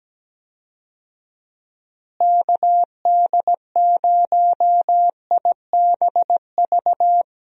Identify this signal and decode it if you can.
Morse code, heard as a keyed tone, about 17 wpm, 700 Hz. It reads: KD0IBV
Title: USA Call Signs - 6 Characters - Repeated 2x - 17wpm